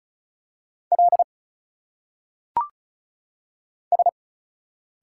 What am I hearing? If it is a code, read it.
Morse code, heard as a keyed tone, about 36 wpm, 700 Hz. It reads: L S
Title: Single Letters - Phonetic Pronunciation 35wpm